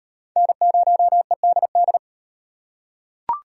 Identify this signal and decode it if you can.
Morse code, heard as a keyed tone, about 38 wpm, 700 Hz. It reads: N0EDD